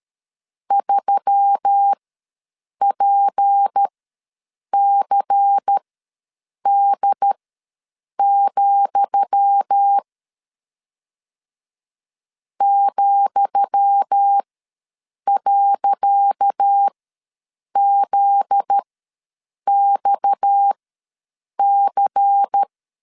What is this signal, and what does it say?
Morse code, heard as a keyed tone, about 13 wpm, spaced as for 7 wpm, 800 Hz. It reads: 3PCD, ,.ZXC
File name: LEZIONI / 10 SIGNS / 10D COMA ws 12wpm 3618